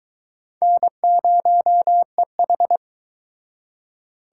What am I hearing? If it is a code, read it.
Morse code, heard as a keyed tone, about 22 wpm, 700 Hz. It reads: N0EH